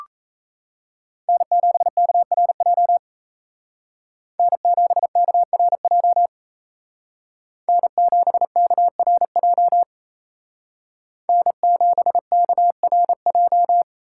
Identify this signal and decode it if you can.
Morse code, heard as a keyed tone, about 35 wpm, 700 Hz. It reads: N7KRJ N7KRJ N7KRJ N7KRJ